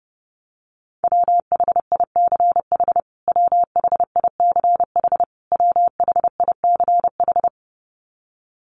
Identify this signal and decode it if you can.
Morse code, heard as a keyed tone, about 30 wpm, 700 Hz. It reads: WHICH WHICH WHICH
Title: Top 500 Words Repeated 30wpm